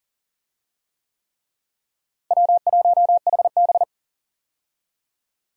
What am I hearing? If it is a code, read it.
Morse code, heard as a keyed tone, about 40 wpm, 700 Hz. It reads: W1HB
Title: USA Call Signs - 4 Characters - Repeated 3x - 40wpm